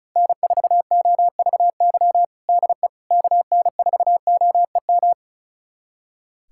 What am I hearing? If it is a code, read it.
Morse code, heard as a keyed tone, about 36 wpm, 700 Hz. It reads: N4OVY DE KN4OEM
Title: USA Callsign DE Callsign 35wpm